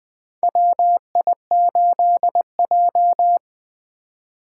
Morse code, heard as a keyed tone, about 20 wpm, 700 Hz. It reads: WI8J
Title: 2023 SOTA Chaser Call Signs - 20wpm